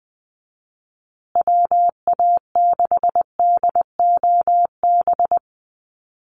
Morse code, heard as a keyed tone, about 20 wpm, 700 Hz. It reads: WA6DOB
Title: USA Call Signs Repeated 2x 20wpm